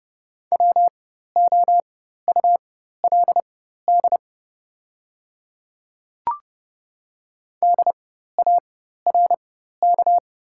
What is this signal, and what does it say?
Morse code, heard as a keyed tone, about 29 wpm, spaced as for 14 wpm, 700 Hz. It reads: WOULD DARK
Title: 4x Character Spacing - Top 500 Words - 30wpm